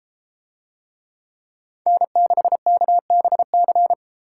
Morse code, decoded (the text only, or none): N6KBC